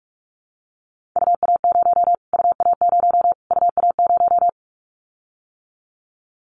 Morse code, heard as a keyed tone, about 44 wpm, 700 Hz. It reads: UA0 UA0 UA0
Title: Callsign Prefixes for Top 25 DXCC - Repeated 3x - 45wpm